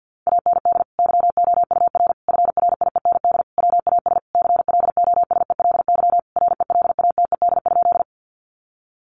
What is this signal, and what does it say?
Morse code, heard as a keyed tone, about 50 wpm, 700 Hz. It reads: AND YOUR FRIEND WAS CLOSELY RELATED?
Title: Sentences from Top 1400 Words 50wpm